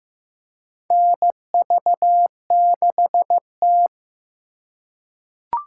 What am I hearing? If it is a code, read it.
Morse code, heard as a keyed tone, about 15 wpm, 700 Hz. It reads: NV6T
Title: USA Call Signs - 4x Character Spacing - 15wpm